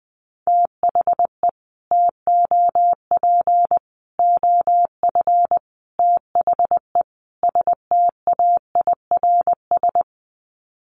Morse code, heard as a keyed tone, about 20 wpm, 700 Hz. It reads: THE TOP OF THE STAIRS
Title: Sets of 5 Words Encore 20wpm